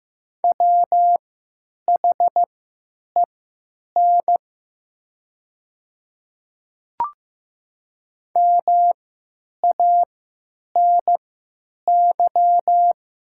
Morse code, heard as a keyed tone, about 15 wpm, spaced as for 9 wpm, 700 Hz. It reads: WHEN MANY
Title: 3x Character Spacing - Top 300 Words - 15wpm